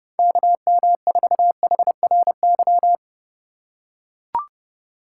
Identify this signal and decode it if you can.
Morse code, heard as a keyed tone, about 32 wpm, 700 Hz. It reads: KM4HRY
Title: USA Call Signs - 4x Character Spacing - 30wpm